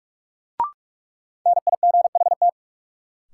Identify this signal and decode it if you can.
Morse code, heard as a keyed tone, about 46 wpm, 700 Hz. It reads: NIGHT
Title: Sets of 4 Words - New Word Drill - 45wpm